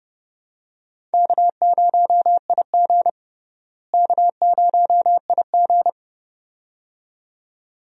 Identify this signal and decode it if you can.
Morse code, heard as a keyed tone, about 32 wpm, 700 Hz. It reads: K0IG K0IG
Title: USA Call Signs - 4 Characters - Repeated 2x - 30wpm